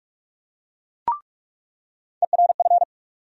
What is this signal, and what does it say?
Morse code, heard as a keyed tone, about 41 wpm, 700 Hz. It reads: ERF
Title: USA Call Sign Postfixes - 3 Letters 45wpm